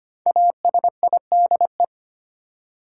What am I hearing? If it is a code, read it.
Morse code, heard as a keyed tone, about 25 wpm, 700 Hz. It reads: ASIDE
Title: Five Letter Words - 25wpm